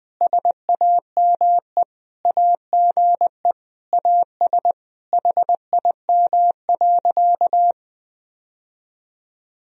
Morse code, heard as a keyed tone, about 21 wpm, 700 Hz. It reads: SAME AGE AS HIM.
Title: Sentences from Top 1000 Words - Review Part 3 - 20wpm